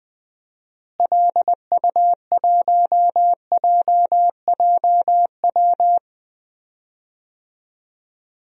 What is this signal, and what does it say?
Morse code, heard as a keyed tone, about 20 wpm, 700 Hz. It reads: LU1JJW
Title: DXCC-100 Call Signs - 6 Characters - Repeated 3x - 20wpm